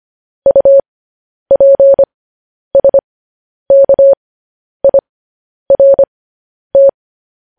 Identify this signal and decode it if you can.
Morse code, heard as a keyed tone, about 25 wpm, spaced as for 10 wpm, 550 Hz. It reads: UPSKIRT
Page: news-headlines-in-morse-code-at-10-wpm